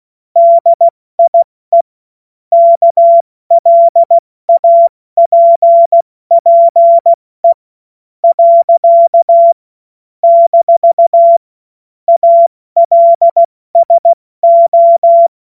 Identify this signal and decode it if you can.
Morse code, heard as a keyed tone, about 16 wpm, spaced as for 13 wpm, 680 Hz. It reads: DIE KLAPPE. - ALSO
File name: Morsekurs / Witze / Witz 24. August 2021, 70 F80